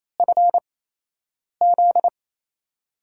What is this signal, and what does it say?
Morse code, heard as a keyed tone, about 28 wpm, spaced as for 8 wpm, 700 Hz. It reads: FZ
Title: USA Call Signs - 8x Character Spacing - 28wpm